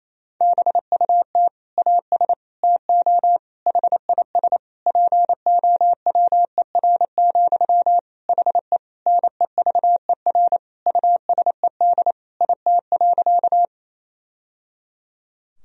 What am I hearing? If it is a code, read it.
Morse code, heard as a keyed tone, about 28 wpm, 700 Hz. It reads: BUT AS TO HIS POWER, HE NEVER USED IT.
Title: Sentences from Top 1000 Words - Review Part 2 - 28wpm